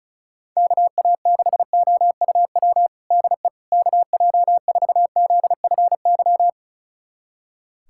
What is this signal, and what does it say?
Morse code, heard as a keyed tone, about 36 wpm, 700 Hz. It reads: KA6OUW DE KJ4ZFY
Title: USA Callsign DE Callsign - 6 Characters - 35wpm